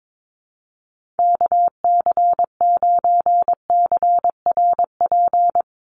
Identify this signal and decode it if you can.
Morse code, heard as a keyed tone, about 22 wpm, 700 Hz. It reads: KC9CRP
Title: USA Call Signs 22wpm